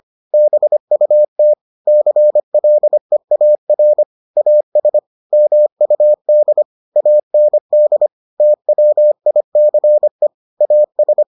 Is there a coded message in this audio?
Yes: Morse code from 0.3 to 11.2 s, heard as a keyed tone, about 25 wpm, 600 Hz. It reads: BUT CLEAR AS MUD AND TWICE AS